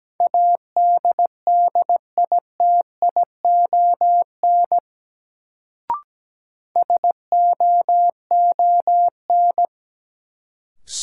Morse code, heard as a keyed tone, about 17 wpm, 700 Hz. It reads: ADDITION SOON
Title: Sets of 3 Words - New Word Drill - 17wpm